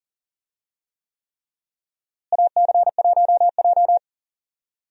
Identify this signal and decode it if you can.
Morse code, heard as a keyed tone, about 40 wpm, 700 Hz. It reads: AC1J
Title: USA Call Signs - 4 Characters - Repeated 3x - 40wpm